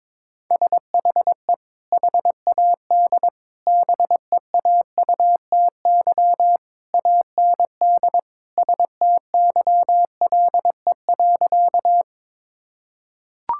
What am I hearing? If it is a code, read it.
Morse code, heard as a keyed tone, about 22 wpm, 700 Hz. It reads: SHE HAD BEAUTY AND STYLE.
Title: Sentences from Top 1000 Words - Review Part 5 - 22wpm